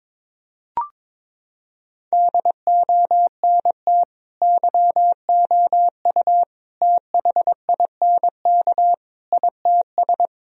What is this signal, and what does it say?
Morse code, heard as a keyed tone, about 22 wpm, 700 Hz. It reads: DONT YOU THINK ITS
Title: Sentences from Top 1000 Words - Review Part 4 - 22wpm